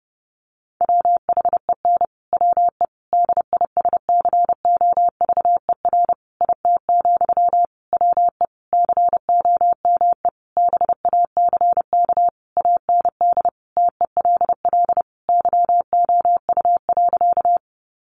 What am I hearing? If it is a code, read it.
Morse code, heard as a keyed tone, about 30 wpm, 700 Hz. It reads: WHEN WE DISCOVER IT, WE COME BACK AND TELL YOU.
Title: Sentences from Top 1100 Words 30wpm